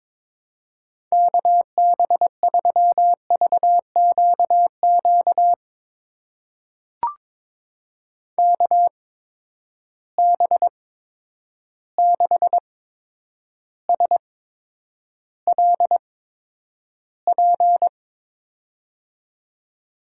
Morse code, heard as a keyed tone, about 22 wpm, spaced as for 9 wpm, 700 Hz. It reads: KB3VQQ KB6SLP
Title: USA Call Signs - 6 Characters - 8x Character Spacing - 22wpm